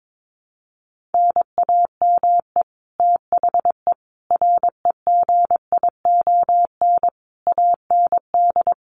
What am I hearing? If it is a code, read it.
Morse code, heard as a keyed tone, about 22 wpm, 700 Hz. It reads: NAME THE REGION AND
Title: Sentences from Top 1500 Words 22wpm